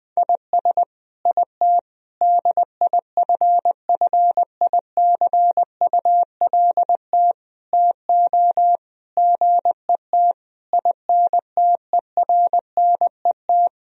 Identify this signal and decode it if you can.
Morse code, heard as a keyed tone, about 20 wpm, 700 Hz. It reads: IS IT DIFFICULT TO GET INTERNET
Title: Sentences from Top 1000 Words - Review Part 4 - 20wpm